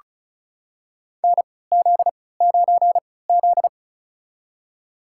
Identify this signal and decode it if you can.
Morse code, heard as a keyed tone, about 36 wpm, spaced as for 20 wpm, 700 Hz. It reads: NZ9Z
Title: USA Call Signs - 4 Characters - 3x Character Spacing - 35wpm